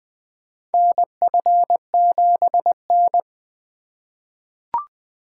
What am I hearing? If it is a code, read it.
Morse code, heard as a keyed tone, about 20 wpm, 700 Hz. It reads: NF7N
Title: USA Call Signs - 4 Characters - 3x Character Spacing - 20wpm